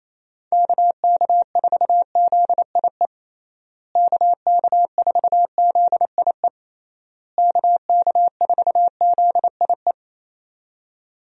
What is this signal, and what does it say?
Morse code, heard as a keyed tone, about 28 wpm, 700 Hz. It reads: KK4ZIE KK4ZIE KK4ZIE
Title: USA Call Signs - 6 Characters - Repeated 3x - 28wpm